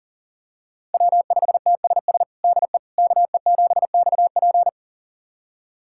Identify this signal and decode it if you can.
Morse code, heard as a keyed tone, about 37 wpm, 700 Hz. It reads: W5TSS DE KE7XP
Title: USA Callsign DE Callsign 40wpm